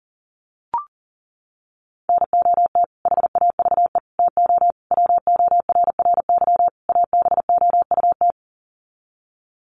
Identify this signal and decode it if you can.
Morse code, heard as a keyed tone, about 40 wpm, 700 Hz. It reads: NOT HAVE TO WORRY ABOUT
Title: Sets of 5 Words 40wpm